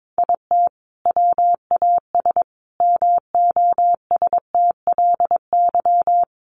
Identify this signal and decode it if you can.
Morse code, heard as a keyed tone, about 22 wpm, 700 Hz. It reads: IT WAS MOSTLY
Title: Sentences from Top 1300 Words 22wpm